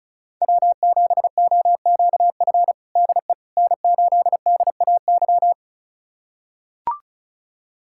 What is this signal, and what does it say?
Morse code, heard as a keyed tone, about 36 wpm, 700 Hz. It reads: W7OQF DE N8DAY
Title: USA Callsign DE Callsign - 5 Characters - 35wpm